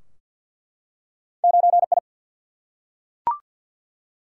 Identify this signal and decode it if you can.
Morse code, heard as a keyed tone, about 47 wpm, 700 Hz. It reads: 9I
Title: Single-Letter Single-Number 50wpm